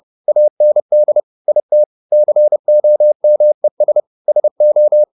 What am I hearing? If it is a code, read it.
Morse code, heard as a keyed tone, about 30 wpm, 600 Hz. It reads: AND IT COMES SO
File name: the old man and the sea / the old man and the sea 1 30wpm